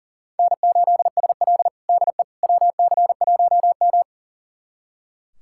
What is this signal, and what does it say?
Morse code, heard as a keyed tone, about 40 wpm, 700 Hz. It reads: N8SL DE WC1M
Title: USA Callsign DE Callsign - 4 Characters - 40wpm